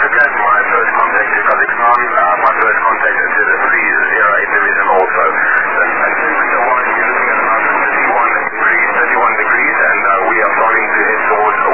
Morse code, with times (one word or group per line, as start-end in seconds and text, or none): none